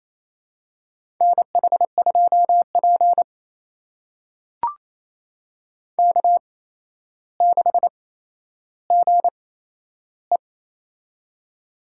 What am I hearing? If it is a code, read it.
Morse code, heard as a keyed tone, about 27 wpm, spaced as for 12 wpm, 700 Hz. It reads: NH2P K6GE
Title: USA Call Signs - 4 Characters - 8x Character Spacing - 28wpm